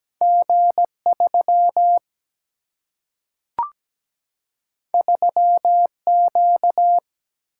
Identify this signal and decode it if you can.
Morse code, heard as a keyed tone, about 17 wpm, 700 Hz. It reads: G3 3Q
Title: Single-Letter Single-Number 17wpm